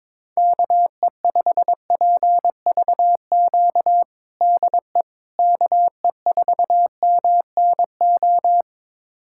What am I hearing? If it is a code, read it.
Morse code, heard as a keyed tone, about 22 wpm, 700 Hz. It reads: KE5PVQ DE KE4MNO
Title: USA Callsign DE Callsign 22wpm